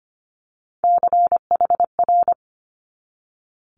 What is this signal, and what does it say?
Morse code, heard as a keyed tone, about 25 wpm, 700 Hz. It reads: CHR
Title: Multi-letter Phonemes 25wpm